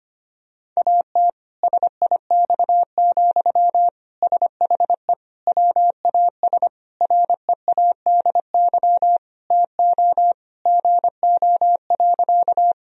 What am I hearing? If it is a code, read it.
Morse code, heard as a keyed tone, about 25 wpm, 700 Hz. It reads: AT SIX, SHE WAS READY TO GO.
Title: Sentences from Top 1000 Words - Review Part 2 - 25wpm